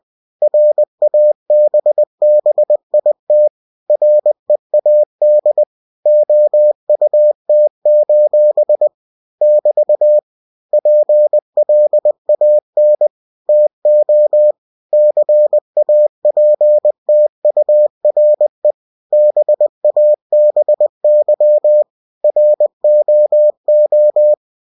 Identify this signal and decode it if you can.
Morse code, heard as a keyed tone, about 20 wpm, 600 Hz. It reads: RABBIT READ OUT: <BT> PLAN TO CAPTURE BABY ROO
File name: cw-books / winnie-the-pooh / winnie-20wpm-page-0063